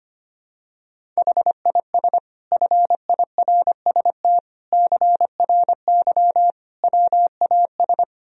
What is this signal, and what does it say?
Morse code, heard as a keyed tone, about 25 wpm, 700 Hz. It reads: HIS FIRST CRY WAS
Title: Sentences from Top 1000 Words - Review Part 2 - 25wpm